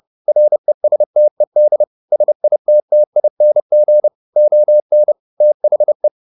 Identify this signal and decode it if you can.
Morse code, heard as a keyed tone, about 30 wpm, 600 Hz. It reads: RESTED SITTING ON THE
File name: the old man and the sea / the old man and the sea 1 30wpm